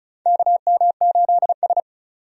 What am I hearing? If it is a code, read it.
Morse code, heard as a keyed tone, about 36 wpm, 700 Hz. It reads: KM8S
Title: USA Call Signs - 4 Characters - 35wpm